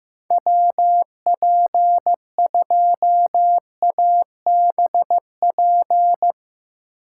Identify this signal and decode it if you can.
Morse code, heard as a keyed tone, about 15 wpm, 700 Hz. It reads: WP2ABP
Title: USA Call Signs - 6 Characters - 4x Character Spacing - 15wpm